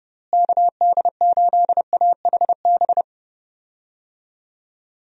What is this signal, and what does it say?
Morse code, heard as a keyed tone, about 30 wpm, 700 Hz. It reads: KD8AHB
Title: USA Call Signs - 6 Characters - Repeated 2x - 30wpm